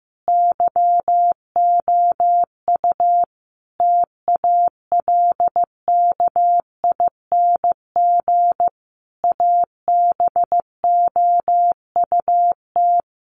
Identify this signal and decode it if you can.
Morse code, heard as a keyed tone, about 15 wpm, 700 Hz. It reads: YOU TALKING ABOUT
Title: Sets of 5 Words 15wpm